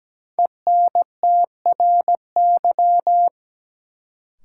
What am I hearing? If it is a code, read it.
Morse code, heard as a keyed tone, about 17 wpm, 700 Hz. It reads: ENTRY
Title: Sentences from Top 1500 Words - New Word Drill Alt - 17wpm